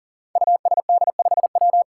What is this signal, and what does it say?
Morse code, heard as a keyed tone, about 38 wpm, 700 Hz. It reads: USD5W
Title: Five Challenging Letter-Number Combinations 40wpm